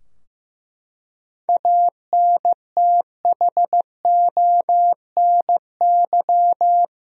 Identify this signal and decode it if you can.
Morse code, heard as a keyed tone, about 15 wpm, 700 Hz. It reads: ANTHONY
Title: USA Common Names - 15wpm